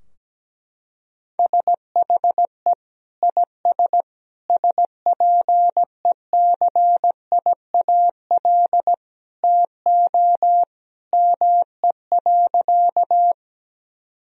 Morse code, heard as a keyed tone, about 17 wpm, 700 Hz. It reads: SHE IS SPECIAL TO ME.
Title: Sentences from Top 1000 Words - Review Part 3 - 17wpm